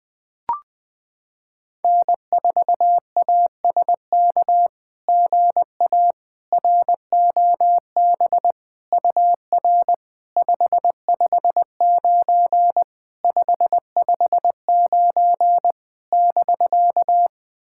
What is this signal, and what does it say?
Morse code, heard as a keyed tone, about 21 wpm, 690 Hz. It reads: N4ASK GA ROB UR 559 559 <BK>
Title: SOTA QSO Practice 20wpm